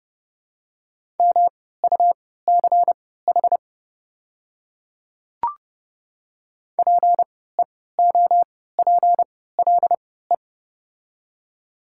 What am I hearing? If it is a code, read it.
Morse code, heard as a keyed tone, about 30 wpm, spaced as for 17 wpm, 700 Hz. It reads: MUCH PEOPLE
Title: 3x Character Spacing - Top 200 Words - 30wpm